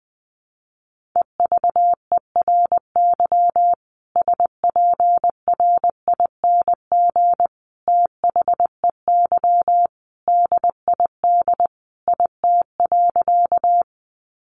Morse code, heard as a keyed tone, about 20 wpm, 700 Hz. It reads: EVERY SPRING THEY DID IT.
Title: Sentences from Top 1200 Words 20wpm